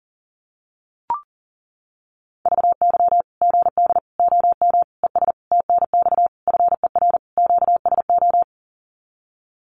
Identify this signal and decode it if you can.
Morse code, heard as a keyed tone, about 37 wpm, 700 Hz. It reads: VY GD OM ES TNX FER QSO
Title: Common QSO Phrases 40wpm